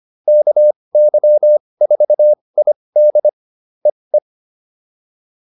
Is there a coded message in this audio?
Yes: Morse code from 0.3 to 4.2 s, heard as a keyed tone, about 25 wpm, spaced as for 20 wpm, 600 Hz. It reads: KY4ID EE